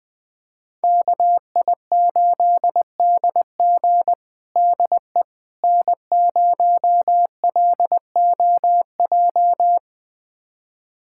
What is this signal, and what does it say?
Morse code, heard as a keyed tone, about 20 wpm, 700 Hz. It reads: KI8DG DE N0LOJ
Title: USA Callsign DE Callsign - 5 Characters - 20wpm